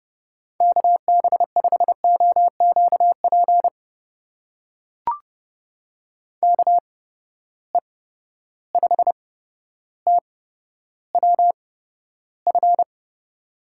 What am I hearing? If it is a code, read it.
Morse code, heard as a keyed tone, about 30 wpm, spaced as for 13 wpm, 700 Hz. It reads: KB5OQP KE5TWF
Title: USA Call Signs - 6 Characters - 8x Character Spacing - 30wpm